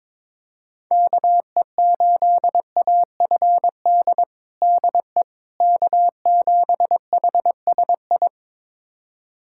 Morse code, heard as a keyed tone, about 21 wpm, 700 Hz. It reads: KE8AFD DE K7HSI
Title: USA Callsign DE Callsign 22wpm